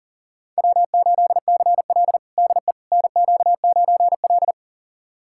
Morse code, heard as a keyed tone, about 37 wpm, 700 Hz. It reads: W8CL DE NQ9L
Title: USA Callsign DE Callsign - 4 Characters - 40wpm